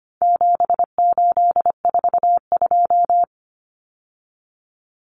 Morse code, heard as a keyed tone, about 25 wpm, 700 Hz. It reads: 7842
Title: Numbers - 4-Digit 25wpm